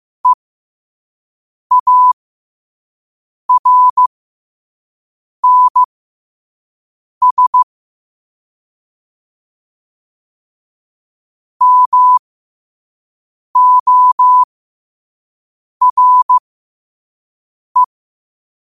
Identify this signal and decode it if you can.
Morse code, heard as a keyed tone, about 15 wpm, spaced as for 5 wpm, 1000 Hz. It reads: EARNS MORE
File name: qotdfiles / 5wpm